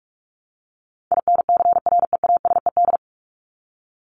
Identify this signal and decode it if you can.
Morse code, heard as a keyed tone, about 44 wpm, 700 Hz. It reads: INCREASED
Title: ED Suffix - Top 100 Words with Base Words 45wpm